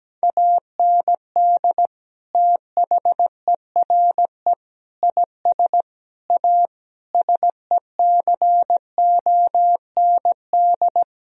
Morse code, heard as a keyed tone, about 17 wpm, 700 Hz. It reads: AND THERE IS A SECOND
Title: Sentences from Top 1000 Words - Review Part 1 - 17wpm